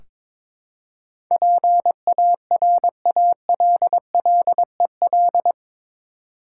Morse code, heard as a keyed tone, about 22 wpm, 700 Hz. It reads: PARALLEL
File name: practice / 22wpm / Eight Letter Words - 22wpm